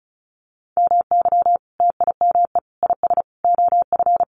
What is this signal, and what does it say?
Morse code, heard as a keyed tone, about 34 wpm, 700 Hz. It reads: MY TIME IS OF
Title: Sentences from Top 600 Words 35wpm